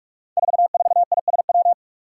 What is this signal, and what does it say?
Morse code, heard as a keyed tone, about 46 wpm, 700 Hz. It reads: 44ISW
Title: Five Challenging Letter-Number Combinations 45wpm